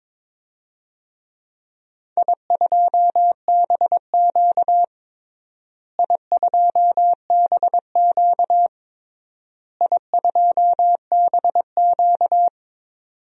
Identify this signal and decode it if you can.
Morse code, heard as a keyed tone, about 22 wpm, 700 Hz. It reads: I2BQ I2BQ I2BQ
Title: DXCC-100 Call Signs - 4 Characters - Repeated 3x - 22wpm